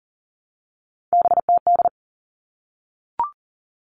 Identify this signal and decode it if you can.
Morse code, heard as a keyed tone, about 40 wpm, 700 Hz. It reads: BTD
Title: All Cut Numbers - 3-Digit 40wpm